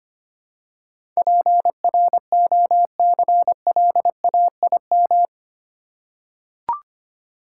Morse code, heard as a keyed tone, about 25 wpm, 700 Hz. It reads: PROCLAIM
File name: Eight Letter Words - 25wpm